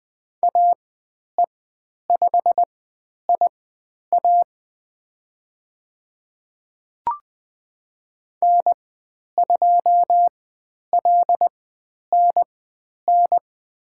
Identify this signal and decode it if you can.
Morse code, heard as a keyed tone, about 20 wpm, spaced as for 10 wpm, 700 Hz. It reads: AE5IA N2LNN
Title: 102 Lesson 17g - Letter u - Call Signs 20wpm